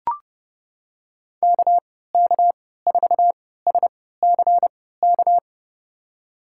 Morse code, heard as a keyed tone, about 30 wpm, spaced as for 17 wpm, 700 Hz. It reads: KK4SCK
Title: USA Call Signs - 3x Character Spacing - 30wpm